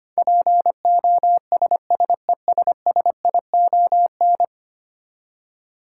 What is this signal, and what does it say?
Morse code, heard as a keyed tone, about 25 wpm, 700 Hz. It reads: POSSESSION
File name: practice / 25wpm / Ten Letter Words - 25wpm